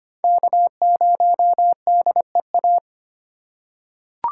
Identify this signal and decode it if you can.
Morse code, heard as a keyed tone, about 25 wpm, 700 Hz. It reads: K0DEA
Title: USA Call Signs - 5 Characters - 4x Character Spacing - 25wpm